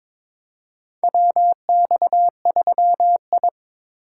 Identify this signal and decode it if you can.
Morse code, heard as a keyed tone, about 22 wpm, 700 Hz. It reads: WX3I
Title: USA Call Signs - 3x Character Spacing - 22wpm